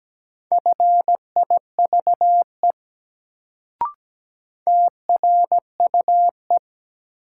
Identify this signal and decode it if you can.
Morse code, heard as a keyed tone, about 17 wpm, 700 Hz. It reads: FIVE TRUE
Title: Sentences from Top 400 Words - New Word Drill - 17wpm